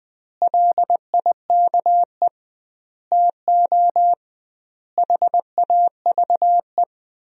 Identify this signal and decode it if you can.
Morse code, heard as a keyed tone, about 20 wpm, 700 Hz. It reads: LIKE TO HAVE
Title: 2x Word-Spacing - Sentences from Top 100 Words - 20wpm